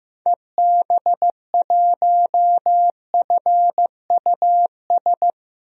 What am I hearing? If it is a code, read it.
Morse code, heard as a keyed tone, about 15 wpm, 700 Hz. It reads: EB1FUS
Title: DXCC-100 Call Signs - 6 Characters - 15wpm